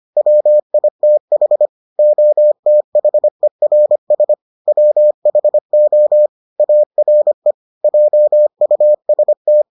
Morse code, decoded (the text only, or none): WITH OTHERS WHO ARE JUST